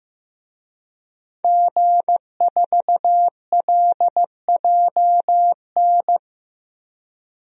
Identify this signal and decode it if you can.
Morse code, heard as a keyed tone, about 16 wpm, 700 Hz. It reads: G4LJN
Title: DXCC-100 Call Signs - 5 Characters - Repeated 2x - 15wpm